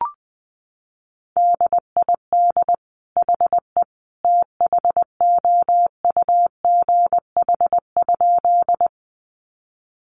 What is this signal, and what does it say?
Morse code, heard as a keyed tone, about 20 wpm, 700 Hz. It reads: DID HE THOUGH?
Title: Sentences from Top 400 Words 20wpm